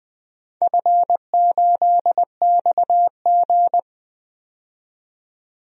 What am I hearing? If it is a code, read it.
Morse code, heard as a keyed tone, about 20 wpm, 700 Hz. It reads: F8XG
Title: DXCC-100 Call Signs - 4 Characters - 20wpm